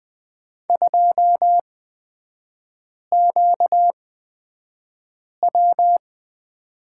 Morse code, heard as a keyed tone, about 20 wpm, spaced as for 5 wpm, 700 Hz. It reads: 2QW
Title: 250 Lesson 40e - BK - Three Characters 20wpm